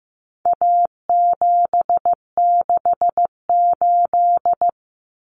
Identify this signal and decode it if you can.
Morse code, heard as a keyed tone, about 15 wpm, 700 Hz. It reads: A768
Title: Common Cut Numbers - 4-Digit 15wpm